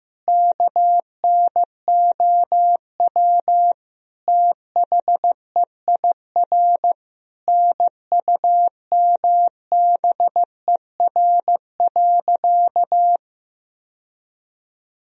Morse code, heard as a keyed tone, about 15 wpm, 700 Hz. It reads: KNOW THEIR NUMBER.
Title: Sentences from Top 1000 Words - Review Part 1 - 15wpm